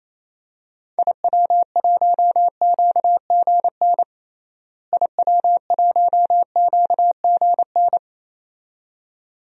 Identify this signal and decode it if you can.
Morse code, heard as a keyed tone, about 27 wpm, 700 Hz. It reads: IW1QGN IW1QGN
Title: DXCC-100 Call Signs - 6 Characters - Repeated 2x - 28wpm